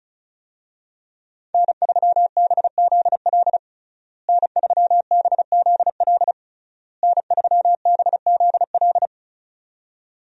Morse code, heard as a keyed tone, about 36 wpm, 700 Hz. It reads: N3BZL N3BZL N3BZL
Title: USA Call Signs - 5 Characters - Repeated 3x - 35wpm